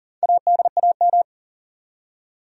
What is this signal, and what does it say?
Morse code, heard as a keyed tone, about 40 wpm, 700 Hz. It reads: ADAM
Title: USA Common Names - Max 5 Letters - 40wpm